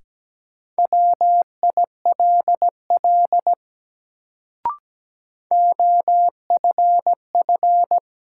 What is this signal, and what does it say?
Morse code, heard as a keyed tone, about 17 wpm, 700 Hz. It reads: WILL OFF
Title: Sets of 2 Words - New Word Drill - 17wpm